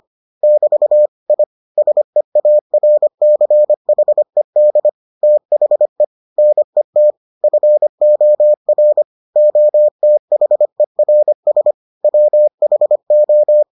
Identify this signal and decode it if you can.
Morse code, heard as a keyed tone, about 25 wpm, 600 Hz. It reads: <BT> I SEARCHED THE NET FOR OTHERS WHO